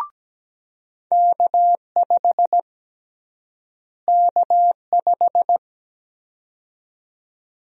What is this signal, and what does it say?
Morse code, heard as a keyed tone, about 17 wpm, 700 Hz. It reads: K5 K5
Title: USA Call Sign Prefixes - Repeated 2x - 17wpm